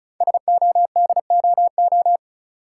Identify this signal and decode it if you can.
Morse code, heard as a keyed tone, about 36 wpm, 700 Hz. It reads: SODOO
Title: Code Groups - 5 Letters - 35wpm